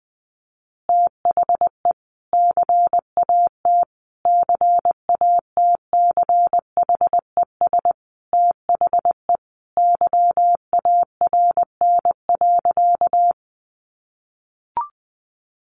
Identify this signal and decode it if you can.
Morse code, heard as a keyed tone, about 20 wpm, 700 Hz. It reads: THE CAT CATCHES THE YARN.
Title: Additional Easy Phrases 20wpm